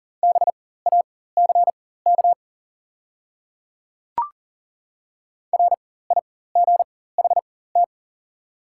Morse code, heard as a keyed tone, about 40 wpm, spaced as for 19 wpm, 690 Hz. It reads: BACK RIGHT
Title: 4x Character Spacing - Top 200 Words - 40wpm